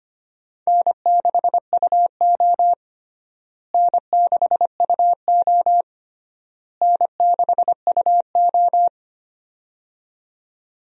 Morse code, heard as a keyed tone, about 25 wpm, 700 Hz. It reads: N6UO N6UO N6UO
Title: USA Call Signs - 4 Characters - Repeated 3x - 25wpm